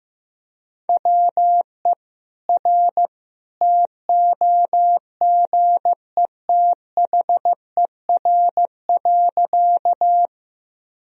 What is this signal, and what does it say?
Morse code, heard as a keyed tone, about 15 wpm, 700 Hz. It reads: WE R TOGETHER.
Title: Sentences from Top 300 Words and Abbreviations 15wpm